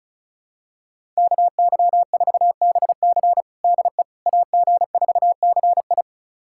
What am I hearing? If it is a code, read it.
Morse code, heard as a keyed tone, about 36 wpm, 700 Hz. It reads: KY4BC DE AG4CI
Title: USA Callsign DE Callsign - 5 Characters - 35wpm